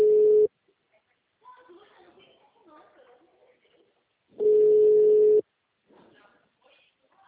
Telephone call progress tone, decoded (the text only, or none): ringback tone